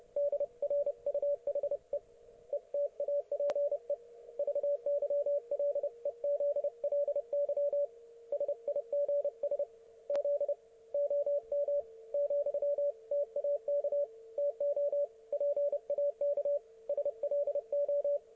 Morse code, heard as a keyed tone, about 30 wpm, 570 Hz, with radio noise. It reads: DRUHE ETAPE VYLEZLY SIGS Z OM, TAK TO PAK SLO
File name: ok1fhi